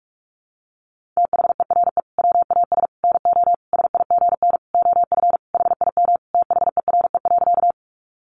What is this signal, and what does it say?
Morse code, heard as a keyed tone, about 44 wpm, 700 Hz. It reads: THERE WAS NO SIGN OF HIM THERE.
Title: Sentences from Top 1000 Words 45wpm